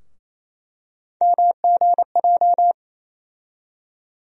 Morse code, heard as a keyed tone, about 27 wpm, 700 Hz. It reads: MGJ
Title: USA Call Sign Postfixes - 3 Letters 28wpm